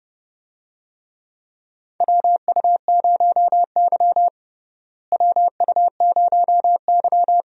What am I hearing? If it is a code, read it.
Morse code, heard as a keyed tone, about 32 wpm, 700 Hz. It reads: WU0Y WU0Y
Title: USA Call Signs - 4 Characters - Repeated 2x - 30wpm